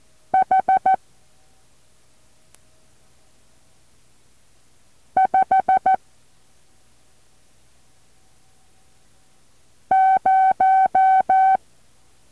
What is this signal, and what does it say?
Morse code, heard as a keyed tone, about 14 wpm, 760 Hz. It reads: H 5 0
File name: intl morse code2